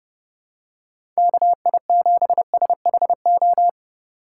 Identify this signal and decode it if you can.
Morse code, heard as a keyed tone, about 30 wpm, 700 Hz. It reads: KI7SHO
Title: USA Call Signs - 6 Characters - 8x Character Spacing - 30wpm